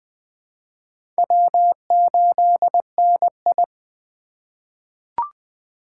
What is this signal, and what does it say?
Morse code, heard as a keyed tone, about 20 wpm, 700 Hz. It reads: W8NI
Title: 2023 POTA Activator Call Signs - 20wpm